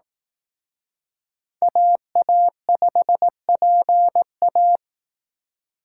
Morse code, heard as a keyed tone, about 18 wpm, 700 Hz. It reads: AA5PA
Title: Speed Racing - USA 2x2 Call Signs 15wpm